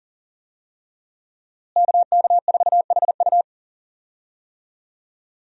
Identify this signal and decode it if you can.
Morse code, heard as a keyed tone, about 40 wpm, 700 Hz. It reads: KK4HU
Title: USA Call Signs - 5 Characters - Repeated 3x - 40wpm